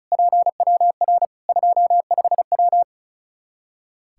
Morse code, heard as a keyed tone, about 36 wpm, 700 Hz. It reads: PWR 25W
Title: Practice QSO - Exchange 2 - Info - 35wpm